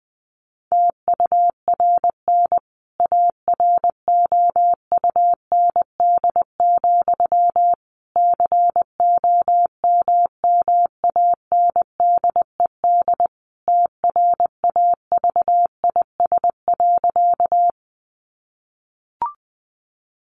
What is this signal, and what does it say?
Morse code, heard as a keyed tone, about 20 wpm, 700 Hz. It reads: TURN AROUND, COMMANDED TRAVIS.
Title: A Sound Of Thunder 20wpm